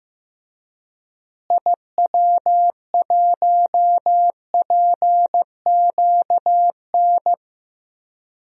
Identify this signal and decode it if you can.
Morse code, heard as a keyed tone, about 15 wpm, 700 Hz. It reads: IW1PQN
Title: DXCC-100 Call Signs - 6 Characters - Repeated 2x - 15wpm